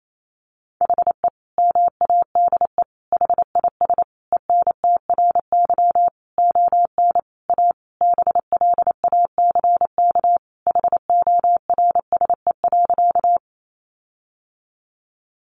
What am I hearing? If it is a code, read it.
Morse code, heard as a keyed tone, about 28 wpm, 700 Hz. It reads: HE MADE HIS ENTRY ON A BLACK HORSE.
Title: Sentences from Top 1500 Words 28wpm